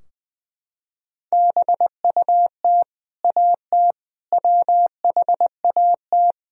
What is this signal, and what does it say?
Morse code, heard as a keyed tone, about 21 wpm, 700 Hz. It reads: BUT AT WHAT
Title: Sentences from Top 1000 Words - Review Part 4 - 20wpm